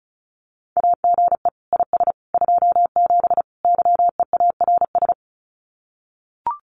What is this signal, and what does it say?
Morse code, heard as a keyed tone, about 36 wpm, 700 Hz. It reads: AGE IS 27 YEARS
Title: Common QSO Phrases Encore 35wpm